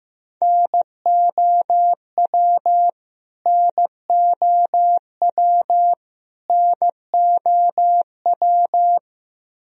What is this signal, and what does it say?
Morse code, heard as a keyed tone, about 15 wpm, 700 Hz. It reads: NOW NOW NOW
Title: Top 100 Words Repeated 15wpm